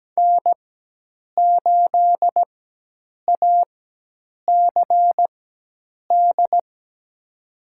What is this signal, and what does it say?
Morse code, heard as a keyed tone, about 17 wpm, spaced as for 8 wpm, 700 Hz. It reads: N8ACD
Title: USA Call Signs - 5 Characters - 4x Character Spacing - 17wpm